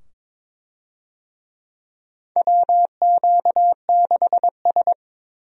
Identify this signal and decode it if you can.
Morse code, heard as a keyed tone, about 22 wpm, 700 Hz. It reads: WQ6S
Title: USA Call Signs - 4 Characters - 22wpm